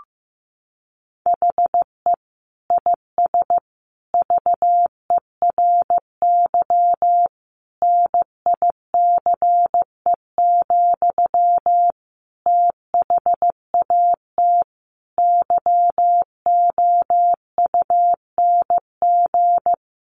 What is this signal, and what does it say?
Morse code, heard as a keyed tone, about 15 wpm, 700 Hz. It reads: HE IS VERY NICE, THAT YOUNG
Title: Sentences from Top 1100 Words 15wpm